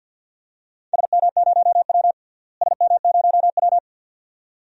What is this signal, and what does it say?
Morse code, heard as a keyed tone, about 51 wpm, 700 Hz. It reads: SM0W SM0W
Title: DXCC-100 Call Signs - 4 Characters - Repeated 2x - 50wpm